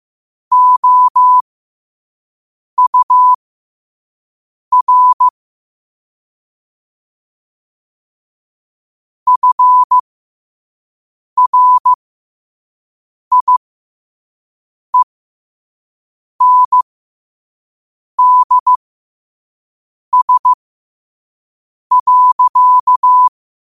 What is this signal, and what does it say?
Morse code, heard as a keyed tone, about 15 wpm, spaced as for 5 wpm, 1000 Hz. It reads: OUR FRIENDS.
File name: qotdfiles / 5wpm